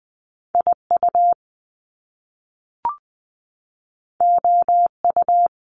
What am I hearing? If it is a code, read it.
Morse code, heard as a keyed tone, about 20 wpm, 700 Hz. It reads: IU OU
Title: Top Two Letter Combinations in Words 20wpm